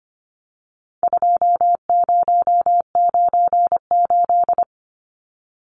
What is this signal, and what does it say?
Morse code, heard as a keyed tone, about 25 wpm, 700 Hz. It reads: 2098
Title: Numbers - 4-Digit 25wpm